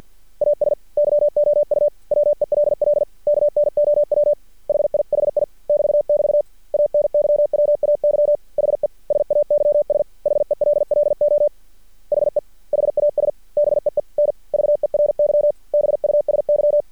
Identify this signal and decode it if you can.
Morse code, heard as a keyed tone, about 50 wpm, 590 Hz. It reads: AS YOU WELL KNOW HIHI <BT><BT> ANYWAY HE SAYS HELLO HE HAS BEEN VERY BUSY
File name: archived / CRSnet